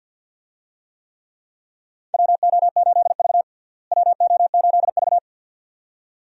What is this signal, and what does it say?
Morse code, heard as a keyed tone, about 52 wpm, 700 Hz. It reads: WO8V WO8V
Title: USA Call Signs - 4 Characters - Repeated 2x - 50wpm